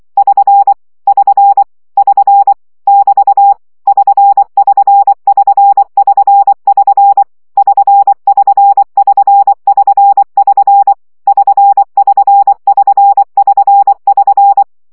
Morse code, heard as a keyed tone, about 24 wpm, 800 Hz. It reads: <SN> <SN> <SN> <BT> <SN><SN><SN><SN><SN> <SN><SN><SN><SN><SN> <SN><SN><SN><SN><SN>
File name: CW gyakorlas sn 20 baud